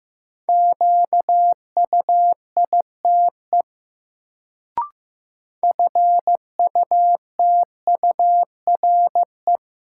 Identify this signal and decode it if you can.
Morse code, heard as a keyed tone, about 15 wpm, 700 Hz. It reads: QUITE FUTURE
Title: Sentences from Top 700 Words - New Word Drill - 15wpm